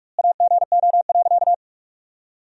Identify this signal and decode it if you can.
Morse code, heard as a keyed tone, about 46 wpm, 700 Hz. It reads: AGO.
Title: 2x Word-Spacing - Sentences from Top 500 Words - 45wpm